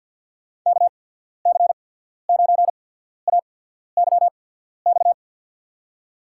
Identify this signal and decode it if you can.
Morse code, heard as a keyed tone, about 49 wpm, 700 Hz. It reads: K C 9 A Y X
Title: USA Call Signs - 6 Characters - 8x Character Spacing - 50wpm